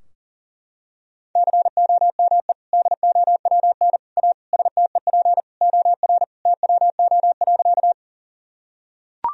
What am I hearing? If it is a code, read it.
Morse code, heard as a keyed tone, about 37 wpm, 700 Hz. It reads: COME DOWN A STEP OR TWO.
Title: Sentences from Top 1000 Words - Review Part 2 - 40wpm